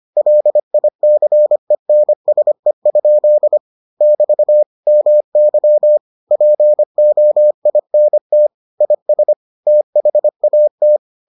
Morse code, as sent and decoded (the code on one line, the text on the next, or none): .-.. .. -.-. -. ... ..--.. / -...- / -- -.-- / .--. --- .. -. - / .. ... / - .... .- -
LICENSE? <BT> MY POINT IS THAT